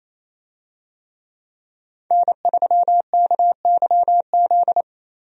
Morse code, heard as a keyed tone, about 28 wpm, 700 Hz. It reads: N3KYZ
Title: USA Call Signs - 5 Characters - 28wpm